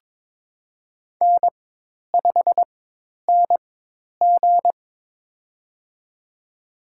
Morse code, heard as a keyed tone, about 22 wpm, spaced as for 10 wpm, 700 Hz. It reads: N5NG
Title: USA Call Signs - 4 Characters - 4x Character Spacing - 22wpm